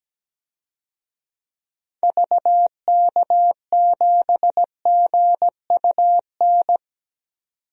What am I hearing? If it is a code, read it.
Morse code, heard as a keyed tone, about 17 wpm, 700 Hz. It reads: VK7GUN
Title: DXCC-100 Call Signs - 6 Characters - 17wpm